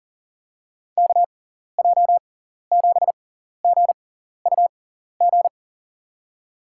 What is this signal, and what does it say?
Morse code, heard as a keyed tone, about 37 wpm, spaced as for 14 wpm, 700 Hz. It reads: KJ7GUG
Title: USA Call Signs - 6x Character Spacing - 40wpm